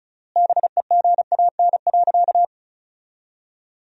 Morse code, heard as a keyed tone, about 36 wpm, 700 Hz. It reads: BEGAN.
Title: 3x Word-Spacing - Sentences from Top 300 Words - 35wpm